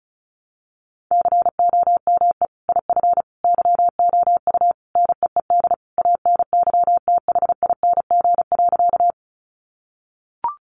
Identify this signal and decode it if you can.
Morse code, heard as a keyed tone, about 34 wpm, 700 Hz. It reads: COME IF YOU NEED ANYTHING.
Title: Sentences from Top 1000 Words 35wpm